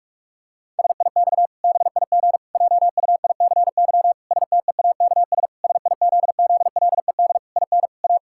Morse code, heard as a keyed tone, about 46 wpm, 700 Hz. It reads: SIX BIG JUICY STEAKS SIZZLED IN A
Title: Pangram Challenge v2 - 45wpm